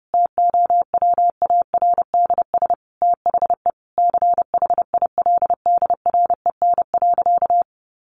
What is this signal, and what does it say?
Morse code, heard as a keyed tone, about 28 wpm, 700 Hz. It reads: TOWARDS THE CHILDREN.
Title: Sentences from Top 1100 Words 30wpm